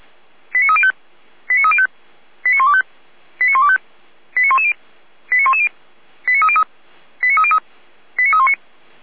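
Five-tone selective call (ZVEI), heard as quick runs of five tones, tones of about 70 ms. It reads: 89305 89305 89125 89125 89110 89110 89303 89303 89319
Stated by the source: ZVEI five-tone selective call received by SDR, from feuerwehr-orlamuende.org fire service.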